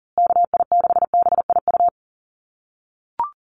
Morse code, heard as a keyed tone, about 40 wpm, 700 Hz. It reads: KI6BIU
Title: USA Call Signs Repeated 3x 40wpm